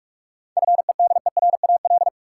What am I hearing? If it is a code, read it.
Morse code, heard as a keyed tone, about 46 wpm, 700 Hz. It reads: FEDERAL
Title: Sets of 3 Words - New Word Drill - 45wpm